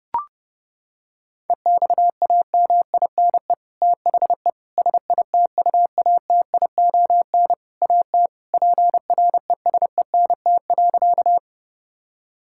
Morse code, heard as a keyed tone, about 30 wpm, 700 Hz. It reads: EXAMINE THE SITUATION AT PRESENT.
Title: Sentences from Top 1600 Words 30wpm